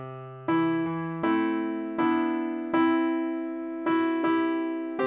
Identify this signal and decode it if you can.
There is no code.